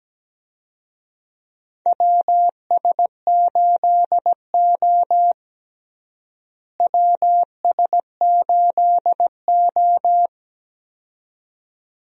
Morse code, heard as a keyed tone, about 17 wpm, 700 Hz. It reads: WS8O WS8O
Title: USA Call Signs - 4 Characters - Repeated 2x - 17wpm